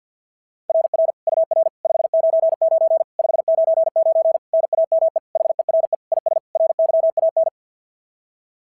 Morse code, heard as a keyed tone, about 48 wpm, 690 Hz. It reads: RR UR 599 599 NAME HERE IS RYAN
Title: SOTA QSO Practice 50wpm